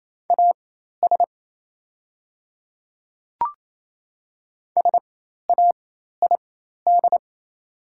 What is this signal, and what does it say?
Morse code, heard as a keyed tone, about 27 wpm, spaced as for 13 wpm, 700 Hz. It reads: AS SAID
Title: 4x Character Spacing - Top 300 Words - 28wpm